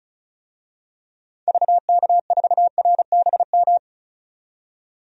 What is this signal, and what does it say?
Morse code, heard as a keyed tone, about 34 wpm, 700 Hz. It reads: VK4RBM
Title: DXCC-100 Call Signs - 6 Characters - 35wpm